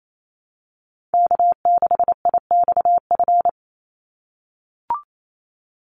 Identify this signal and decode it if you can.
Morse code, heard as a keyed tone, about 28 wpm, 700 Hz. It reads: K6IXF
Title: USA Call Signs 28wpm